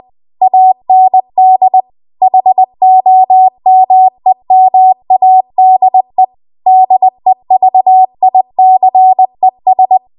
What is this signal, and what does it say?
Morse code, heard as a keyed tone, about 20 wpm, 750 Hz. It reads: AND HOMEMADE DEVICES
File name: Archive / 20 WPM / 20WPM